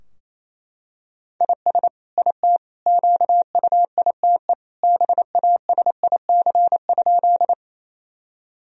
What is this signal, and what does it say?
Morse code, heard as a keyed tone, about 28 wpm, 700 Hz. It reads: IS IT QUITE BASIC?
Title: Sentences from Top 1000 Words - Review Part 3 - 28wpm